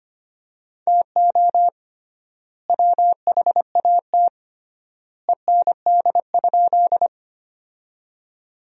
Morse code, heard as a keyed tone, about 25 wpm, 700 Hz. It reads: TO WHAT END?
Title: 3x Word-Spacing - Sentences from Top 200 Words - 25wpm